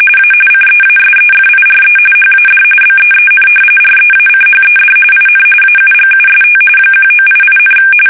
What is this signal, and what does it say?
RTTY, about 45.45 baud. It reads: A CEASEFIRE HAD BEEN REACHED IN AUGUST 1988,
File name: fsk850 / battleshipnj